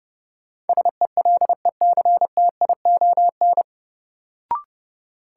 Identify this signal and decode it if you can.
Morse code, heard as a keyed tone, about 32 wpm, 700 Hz. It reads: SELECTION
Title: Sentences from Top 1400 Words - New Word Drill - 30wpm